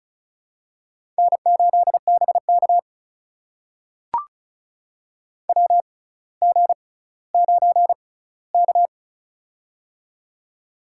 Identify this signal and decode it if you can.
Morse code, heard as a keyed tone, about 36 wpm, spaced as for 18 wpm, 700 Hz. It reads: N8BK WG9K
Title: USA Call Signs - 4 Characters - 6x Character Spacing - 35wpm